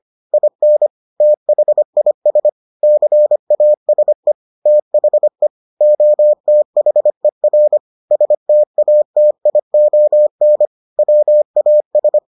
Morse code, heard as a keyed tone, about 25 wpm, 600 Hz. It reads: IN THIS CASE THE OTHER STATION WAS